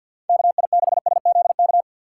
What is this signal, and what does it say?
Morse code, heard as a keyed tone, about 46 wpm, 700 Hz. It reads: KI6SZK
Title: USA Call Signs - 6 Characters - 6x Character Spacing - 50wpm